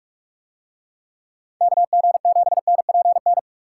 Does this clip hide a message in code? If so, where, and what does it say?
Yes: Morse code from 1.6 to 3.4 s, heard as a keyed tone, about 44 wpm, 700 Hz. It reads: KG7NPN